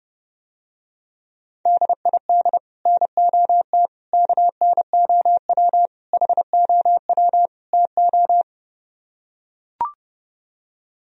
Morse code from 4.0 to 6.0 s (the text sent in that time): KNOW